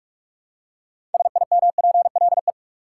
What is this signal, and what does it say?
Morse code, heard as a keyed tone, about 46 wpm, 700 Hz. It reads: SIMPLE
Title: Six Letter Words - 45wpm